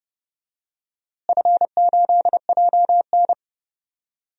Morse code, heard as a keyed tone, about 30 wpm, 700 Hz. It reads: F8JN
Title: DXCC-100 Call Signs - 4 Characters - Repeated 3x - 30wpm